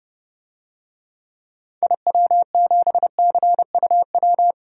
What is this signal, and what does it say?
Morse code, heard as a keyed tone, about 30 wpm, 700 Hz. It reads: IW7CUW
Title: DXCC-100 Call Signs - 6 Characters - 30wpm